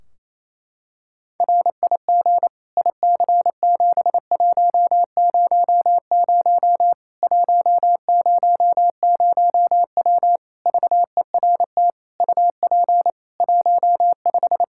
Morse code, heard as a keyed tone, about 28 wpm, 700 Hz. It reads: RIG IC7100 100W VERT UP 15
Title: K1USN SST QSO Practice 28wpm